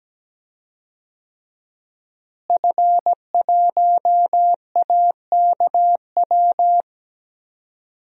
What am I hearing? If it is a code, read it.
Morse code, heard as a keyed tone, about 17 wpm, 700 Hz. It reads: F1AKW
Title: DXCC-100 Call Signs - 5 Characters - Repeated 3x - 17wpm